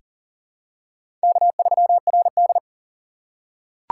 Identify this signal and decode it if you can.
Morse code, heard as a keyed tone, about 40 wpm, 700 Hz. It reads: K3RD
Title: USA Call Signs - 4 Characters - 3x Character Spacing - 40wpm